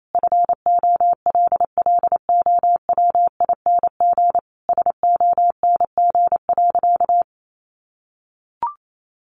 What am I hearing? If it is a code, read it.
Morse code, heard as a keyed tone, about 28 wpm, 700 Hz. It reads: FOLLOWING SONG.
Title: Sentences from Top 900 Words 28wpm